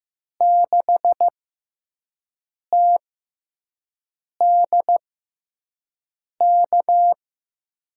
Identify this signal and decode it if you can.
Morse code, heard as a keyed tone, about 15 wpm, spaced as for 5 wpm, 700 Hz. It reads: 6TDK
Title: USA Call Signs - 5 Characters - 6x Character Spacing - 15wpm